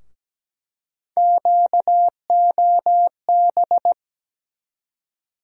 Morse code, heard as a keyed tone, about 17 wpm, 700 Hz. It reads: QOB
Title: USA Call Sign Postfixes - 3 Letters 17wpm